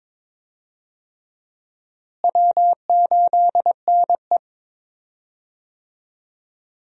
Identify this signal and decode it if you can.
Morse code, heard as a keyed tone, about 22 wpm, 700 Hz. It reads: W8NE
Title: USA Call Signs - 4 Characters - Repeated 3x - 22wpm